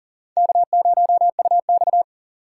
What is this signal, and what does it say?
Morse code, heard as a keyed tone, about 40 wpm, 690 Hz. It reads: K0UX
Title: USA Call Signs - 6x Character Spacing - 40wpm